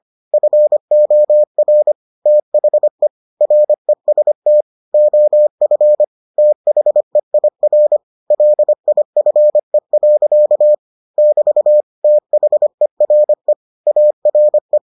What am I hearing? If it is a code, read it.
Morse code, heard as a keyed tone, about 25 wpm, 600 Hz. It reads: FOR THE REST OF THEIR LIFE. <BT> THERE ARE